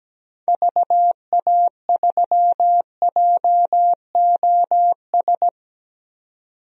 Morse code, heard as a keyed tone, about 17 wpm, 700 Hz. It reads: VA3JOS
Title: DXCC-100 Call Signs - 6 Characters - 17wpm